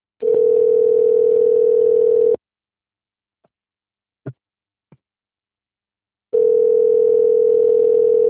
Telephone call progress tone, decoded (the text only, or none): ringback tone